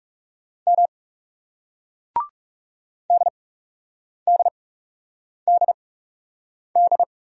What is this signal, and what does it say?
Morse code, heard as a keyed tone, about 40 wpm, spaced as for 8 wpm, 700 Hz. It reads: M DDDD